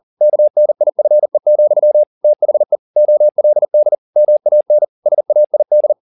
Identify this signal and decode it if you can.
Morse code, heard as a keyed tone, about 40 wpm, 600 Hz. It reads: KNIFE, THE OLD MAN SAID
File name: the old man and the sea 2 40wpm